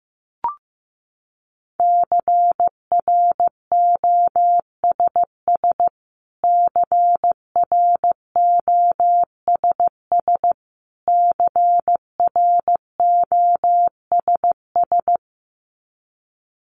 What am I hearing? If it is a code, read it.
Morse code, heard as a keyed tone, about 15 wpm, 700 Hz. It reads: CROSS CROSS CROSS
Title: Top 500 Words Repeated 15wpm